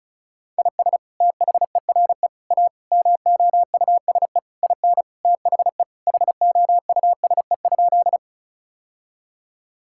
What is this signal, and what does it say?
Morse code, heard as a keyed tone, about 36 wpm, 700 Hz. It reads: IS THERE A MOUSE IN THE HOUSE?
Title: Sentences from Top 1000 Words - Review Part 5 - 35wpm